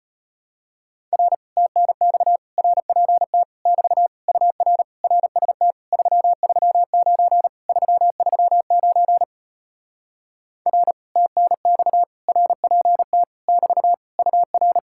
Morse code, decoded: R TNX RPT <BT> UR RST 339 339 R TNX RPT <BT> UR

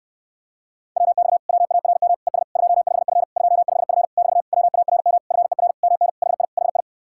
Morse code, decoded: WX WARM ES 25F 25F <BT> WARM FER MN HI HI

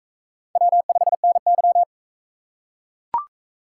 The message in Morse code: W5NY

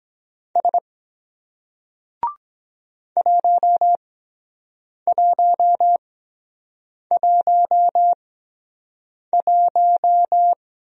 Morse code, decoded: S 1111

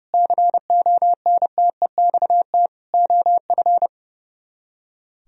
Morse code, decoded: CONTEXT OF

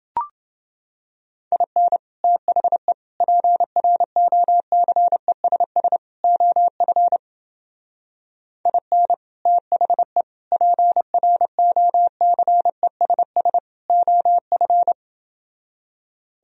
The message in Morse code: IN THE PROCESS OF IN THE PROCESS OF

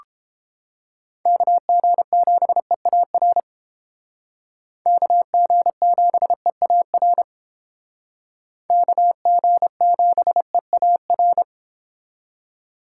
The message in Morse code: KG7EAR KG7EAR KG7EAR

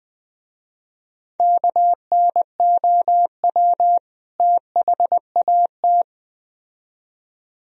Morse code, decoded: KNOW THAT

